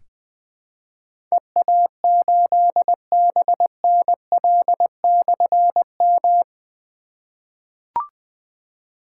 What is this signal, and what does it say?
Morse code, heard as a keyed tone, about 20 wpm, 700 Hz. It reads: EA8BNL/M
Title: Difficult Call Signs 20wpm